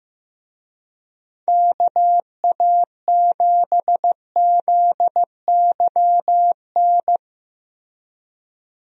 Morse code, heard as a keyed tone, about 15 wpm, 700 Hz. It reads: KA7ZYN
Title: USA Call Signs - 6 Characters - Repeated 2x - 15wpm